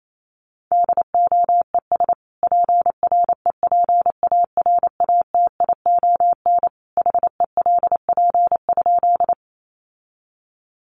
Morse code, heard as a keyed tone, about 28 wpm, 700 Hz. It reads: DOES PREPARATION HELP?
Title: Sentences from Top 1400 Words 28wpm